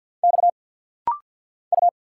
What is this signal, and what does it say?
Morse code, heard as a keyed tone, about 50 wpm, 700 Hz. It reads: X U
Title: Single Letter-Number Alt - Rapid-Fire 50wpm